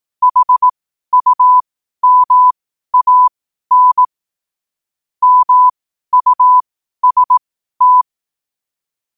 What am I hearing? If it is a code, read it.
Morse code, heard as a keyed tone, about 18 wpm, spaced as for 12 wpm, 1000 Hz. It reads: HUMAN MUST
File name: qotdfiles / 12wpm